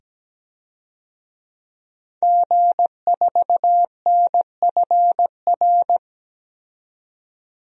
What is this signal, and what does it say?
Morse code, heard as a keyed tone, about 17 wpm, 700 Hz. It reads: G4NFR
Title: DXCC-100 Call Signs - 5 Characters - 17wpm